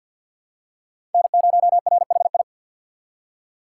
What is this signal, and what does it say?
Morse code, heard as a keyed tone, about 52 wpm, 700 Hz. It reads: N0RHI
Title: USA Call Signs - 5 Characters - 4x Character Spacing - 50wpm